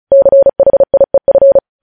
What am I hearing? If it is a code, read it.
Morse code, heard as a keyed tone, about 36 wpm, 550 Hz. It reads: CHIEF
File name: podcast / News headlines-1x-35x35